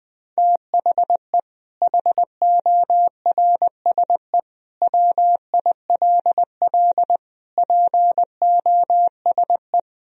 Morse code, heard as a keyed tone, about 21 wpm, 700 Hz. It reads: THE HORSE WILL POSE
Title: Sentences from Top 1000 Words - Review Part 2 - 20wpm